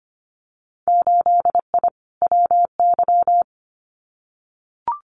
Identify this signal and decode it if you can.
Morse code, heard as a keyed tone, about 25 wpm, 700 Hz. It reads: 8I WY
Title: Winter Field Day Exchanges Encore 25wpm